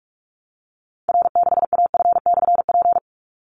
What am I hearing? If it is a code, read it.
Morse code, heard as a keyed tone, about 44 wpm, 700 Hz. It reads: R6AF/P
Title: Call Signs Repeated 3x - 45wpm